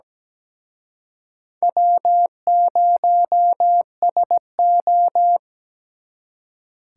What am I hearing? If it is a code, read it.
Morse code, heard as a keyed tone, about 20 wpm, 700 Hz. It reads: W0SO
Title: Speed Racing - USA 1x2 Call Signs 17wpm